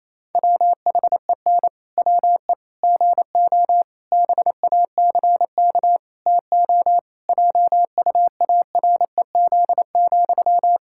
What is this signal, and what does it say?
Morse code, heard as a keyed tone, about 28 wpm, 700 Hz. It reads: WHEN WE GO BACK TO JUAREZ,
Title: Pangram Challenge v2 - 28wpm